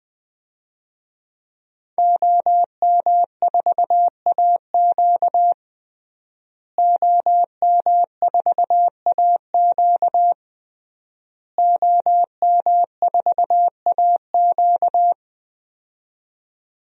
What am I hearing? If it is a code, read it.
Morse code, heard as a keyed tone, about 20 wpm, 700 Hz. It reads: OM4AQ OM4AQ OM4AQ
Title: DXCC-100 Call Signs - 5 Characters - Repeated 3x - 20wpm